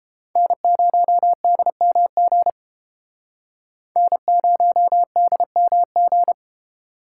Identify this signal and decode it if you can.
Morse code, heard as a keyed tone, about 29 wpm, 700 Hz. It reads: N0DMG N0DMG